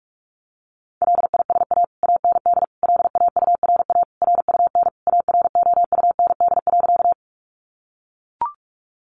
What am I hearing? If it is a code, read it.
Morse code, heard as a keyed tone, about 44 wpm, 700 Hz. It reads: LISA AND LAURA RUN AROUND.
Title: Additional Easy Phrases 45wpm